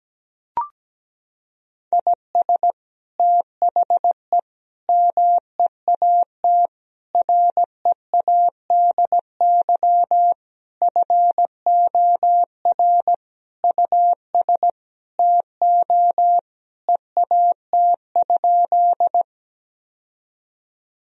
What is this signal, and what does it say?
Morse code, heard as a keyed tone, about 17 wpm, 700 Hz. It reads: IS THE MEAT READY FOR US TO EAT?
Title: Sentences from Top 1000 Words - Review Part 3 - 17wpm